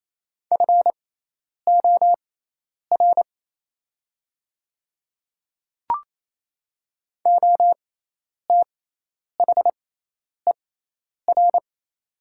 Morse code, heard as a keyed tone, about 28 wpm, spaced as for 10 wpm, 700 Hz. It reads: FOR OTHER
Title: 6x Character Spacing - Top 100 Words - 28wpm